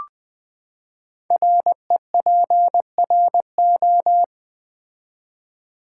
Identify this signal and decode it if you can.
Morse code, heard as a keyed tone, about 20 wpm, 700 Hz. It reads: REPRO
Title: Build A Word - 5 Letters At A Time - 12-Letter Words - 20wpm